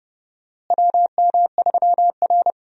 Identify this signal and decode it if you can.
Morse code, heard as a keyed tone, about 29 wpm, 700 Hz. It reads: WM3R
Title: USA Call Signs - 4 Characters - Repeated 3x - 30wpm